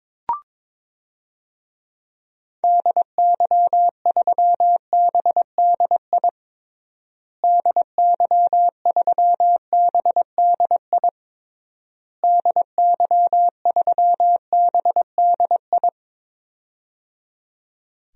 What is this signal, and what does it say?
Morse code, heard as a keyed tone, about 22 wpm, 700 Hz. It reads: DY3BDI DY3BDI DY3BDI
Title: DXCC-100 Call Signs - 6 Characters - Repeated 3x - 22wpm